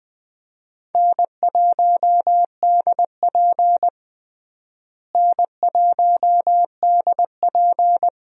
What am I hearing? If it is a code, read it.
Morse code, heard as a keyed tone, about 21 wpm, 700 Hz. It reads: N1DP N1DP
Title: USA Call Signs - 4 Characters - Repeated 2x - 20wpm